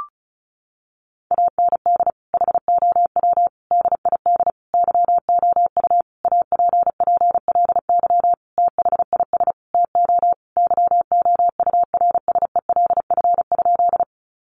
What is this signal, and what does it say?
Morse code, heard as a keyed tone, about 34 wpm, 700 Hz. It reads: AND HOW DID YOU APPLY THIS TO YOURSELF?
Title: Sentences from Top 1200 Words 35wpm